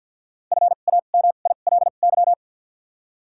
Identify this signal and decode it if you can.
Morse code, heard as a keyed tone, about 54 wpm, spaced as for 37 wpm, 700 Hz. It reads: FAMILY
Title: 2x Character Spacing - Top 500 Words - 50wpm